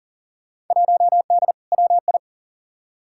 Morse code, heard as a keyed tone, about 40 wpm, 700 Hz. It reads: 1D WI
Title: Field Day Exchanges - 40wpm